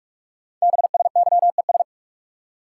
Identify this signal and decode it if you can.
Morse code, heard as a keyed tone, about 44 wpm, 700 Hz. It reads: BSYES
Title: Code Groups - 5 Letters - 45wpm